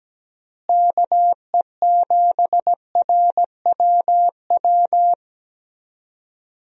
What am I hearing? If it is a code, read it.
Morse code, heard as a keyed tone, about 17 wpm, 700 Hz. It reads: KE7RWW
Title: USA Call Signs - 6 Characters - Repeated 2x - 17wpm